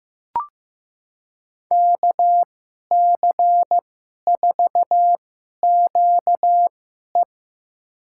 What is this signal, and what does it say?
Morse code, heard as a keyed tone, about 15 wpm, spaced as for 11 wpm, 700 Hz. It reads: KC4QE
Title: USA Call Signs - 5 Characters - 2x Character Spacing - 15wpm